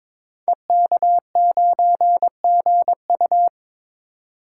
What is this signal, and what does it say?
Morse code, heard as a keyed tone, about 22 wpm, 700 Hz. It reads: EK9GU
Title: Code Groups - 5 Letter:Numbers - 22wpm